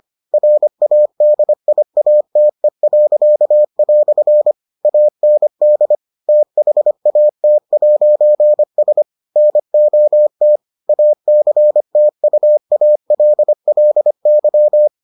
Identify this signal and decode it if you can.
Morse code, heard as a keyed tone, about 25 wpm, 600 Hz. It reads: RADIATE." AND THAT'S NOT ACTUALLY